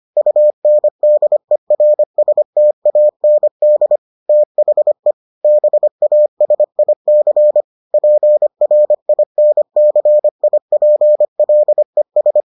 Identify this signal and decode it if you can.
Morse code, heard as a keyed tone, about 25 wpm, 600 Hz. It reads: UNDERSTAND THE BASIC PRINCIPLES